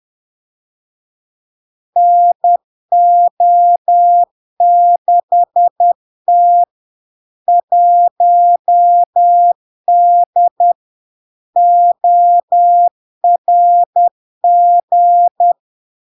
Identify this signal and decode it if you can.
Morse code, heard as a keyed tone, about 10 wpm, 700 Hz. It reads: NO6T 1D ORG
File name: LARK-FD-2022-Log-in-CW-10wpm-700-Hz